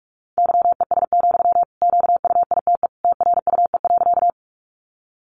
Morse code, heard as a keyed tone, about 45 wpm, 700 Hz. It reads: YES, QUITE TRUE.
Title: Sentences from Top 700 Words 45wpm